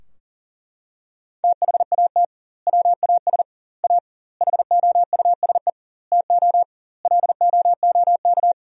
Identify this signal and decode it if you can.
Morse code, heard as a keyed tone, about 40 wpm, 700 Hz. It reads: THAT WAS A HOUSE TO LOOK
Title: 2x Word-Spacing - Sentences from Top 200 Words - 40wpm